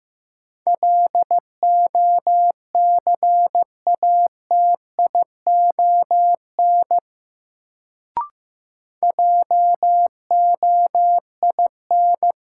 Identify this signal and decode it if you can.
Morse code, heard as a keyed tone, about 15 wpm, 700 Hz. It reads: LOCATION JOIN
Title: Sentences from Top 1000 Words - Review Part 6 - New Word Drill - 15wpm